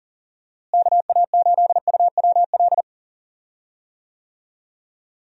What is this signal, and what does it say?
Morse code, heard as a keyed tone, about 40 wpm, 700 Hz. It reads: KA8UWL
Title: USA Call Signs - 6 Characters - Repeated 3x - 40wpm